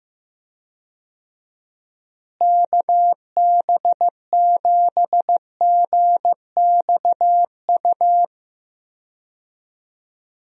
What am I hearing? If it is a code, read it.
Morse code, heard as a keyed tone, about 15 wpm, 700 Hz. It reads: KB7GXU
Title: USA Call Signs - 6 Characters - Repeated 3x - 15wpm